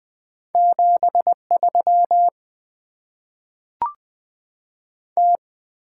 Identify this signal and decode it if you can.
Morse code, heard as a keyed tone, about 20 wpm, 700 Hz. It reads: 7 3 T
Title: Top 20 QSO Elements - 8x Character Spacing - 20wpm